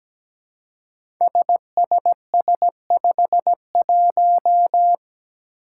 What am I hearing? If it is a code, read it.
Morse code, heard as a keyed tone, about 17 wpm, 700 Hz. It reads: SSS51
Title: Five Challenging Letter-Number Combinations 17wpm